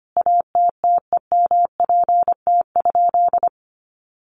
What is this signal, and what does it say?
Morse code, heard as a keyed tone, about 25 wpm, 700 Hz. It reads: ATTEMPT?
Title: Sentences from Top 1000 Words 25wpm